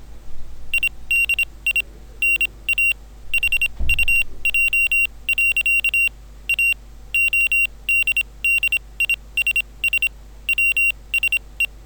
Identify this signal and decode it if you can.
Morse code, heard as a keyed tone, about 26 wpm, spaced as for 21 wpm, 2750 Hz. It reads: IDI NA HUJ. A ODDISS WSE